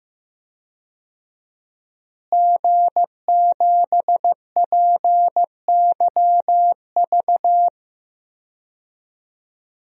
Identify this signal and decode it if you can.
Morse code, heard as a keyed tone, about 15 wpm, 700 Hz. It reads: G7PYV
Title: DXCC-100 Call Signs - 5 Characters - Repeated 2x - 15wpm